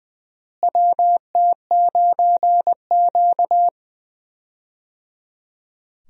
Morse code, heard as a keyed tone, about 21 wpm, 700 Hz. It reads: WT9Q
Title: USA Call Signs - 4 Characters - Repeated 3x - 20wpm